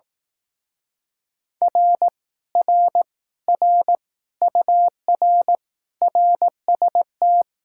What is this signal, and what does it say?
Morse code, heard as a keyed tone, about 18 wpm, 700 Hz. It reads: R R R UR RST
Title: Speed-Racing Common QSO Phrases Encore - 15wpm